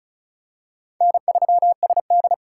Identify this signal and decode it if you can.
Morse code, heard as a keyed tone, about 36 wpm, 700 Hz. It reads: N3SD
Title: USA Call Signs - 4 Characters - 2x Character Spacing - 35wpm